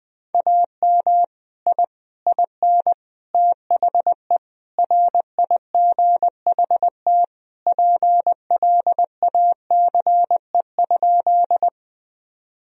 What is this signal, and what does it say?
Morse code, heard as a keyed tone, about 20 wpm, 700 Hz. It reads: AM I IN THE RIGHT PLACE?
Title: Sentences from Top 1000 Words - Review Part 2 - 20wpm